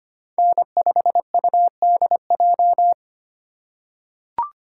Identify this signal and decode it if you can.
Morse code, heard as a keyed tone, about 25 wpm, 700 Hz. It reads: N5UDJ
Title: USA Call Signs - 5 Characters - 3x Character Spacing - 25wpm